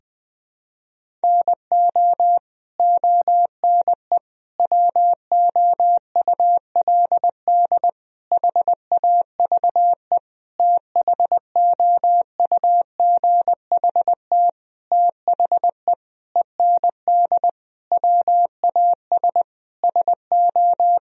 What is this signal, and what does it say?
Morse code, heard as a keyed tone, about 20 wpm, 700 Hz. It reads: NO ONE WOULD HAVE THOUGHT THE END WAS SO
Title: Sentences from Top 1000 Words - Review Part 1 - 20wpm